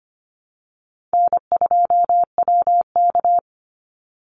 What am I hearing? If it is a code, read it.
Morse code, heard as a keyed tone, about 25 wpm, 700 Hz. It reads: N2WK
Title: Easy Call Signs - 25wpm